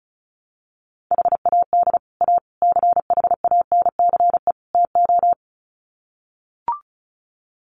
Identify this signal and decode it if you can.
Morse code, heard as a keyed tone, about 34 wpm, 700 Hz. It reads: HAD A CHANCE TO
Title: Sets of 4 Words 35wpm